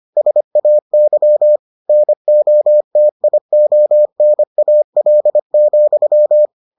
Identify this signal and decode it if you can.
Morse code, heard as a keyed tone, about 25 wpm, 600 Hz. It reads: SAY NOTIONAL,